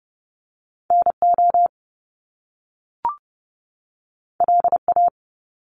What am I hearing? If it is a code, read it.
Morse code, heard as a keyed tone, about 30 wpm, 700 Hz. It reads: NO LA
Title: Top Two Letter Combinations in Words 30wpm